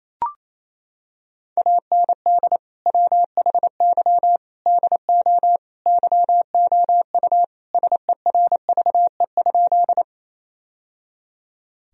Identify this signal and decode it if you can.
Morse code, heard as a keyed tone, about 28 wpm, 690 Hz. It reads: AND WHY DO YOU SERVE?
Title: Sentences from Top 1000 Words - Review Part 2 - 28wpm